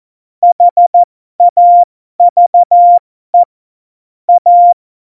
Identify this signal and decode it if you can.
Morse code, heard as a keyed tone, about 14 wpm, spaced as for 12 wpm, 700 Hz. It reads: HAVE A
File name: Morsekurs / Witze